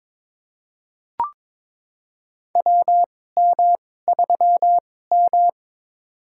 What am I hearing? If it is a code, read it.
Morse code, heard as a keyed tone, about 22 wpm, spaced as for 16 wpm, 700 Hz. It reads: WM3M
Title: USA Call Signs - 2x Character Spacing - 22wpm